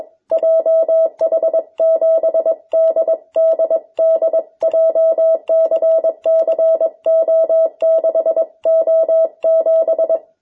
Morse code, heard as a keyed tone, about 21 wpm, 650 Hz. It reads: JH7DDD JCCO6O7
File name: jh8cbh / 2021cw